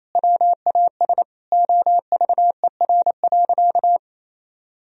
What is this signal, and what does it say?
Morse code, heard as a keyed tone, about 28 wpm, 700 Hz. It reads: WAS OVER.
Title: Sentences from Top 1000 Words - Review Part 4 - 28wpm